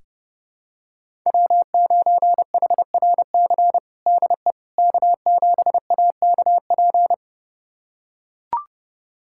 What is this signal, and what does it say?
Morse code, heard as a keyed tone, about 32 wpm, 700 Hz. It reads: W9HRC DE K7AKP
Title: USA Callsign DE Callsign - 5 Characters - 30wpm